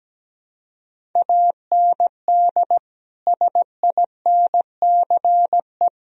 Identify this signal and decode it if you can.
Morse code, heard as a keyed tone, about 17 wpm, 700 Hz. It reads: AND SINCE
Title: Sentences from Top 1000 Words - Review Part 1 - 17wpm